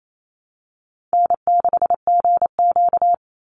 Morse code, heard as a keyed tone, about 28 wpm, 700 Hz. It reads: N6GQ
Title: Easy Call Signs Repeated 2x - 28wpm